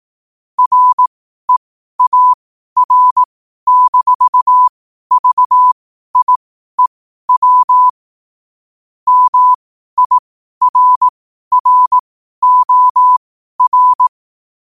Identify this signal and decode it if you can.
Morse code, heard as a keyed tone, about 18 wpm, spaced as for 12 wpm, 1000 Hz. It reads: REAR-VIEW MIRROR
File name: qotdfiles / 12wpm